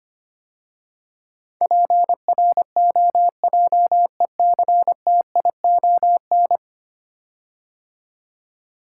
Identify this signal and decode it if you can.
Morse code, heard as a keyed tone, about 25 wpm, 700 Hz. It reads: PROJECTION
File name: 10 Letter Words - First 3 Letters Given - 25wpm